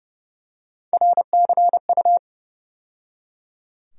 Morse code, heard as a keyed tone, about 31 wpm, 700 Hz. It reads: RCU
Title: USA Call Sign Postfixes - 3 Letters 30wpm